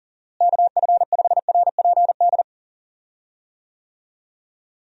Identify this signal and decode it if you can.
Morse code, heard as a keyed tone, about 40 wpm, 700 Hz. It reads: KF5RPD
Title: USA Call Signs - 6 Characters - Repeated 3x - 40wpm